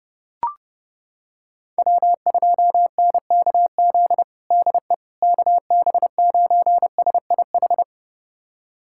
Morse code, heard as a keyed tone, about 32 wpm, 700 Hz. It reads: W2NKZ DE KB9SIH
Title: USA Callsign DE Callsign 30wpm